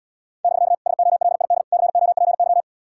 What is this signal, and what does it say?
Morse code, heard as a keyed tone, about 76 wpm, 700 Hz. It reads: <BK> SOLID COPY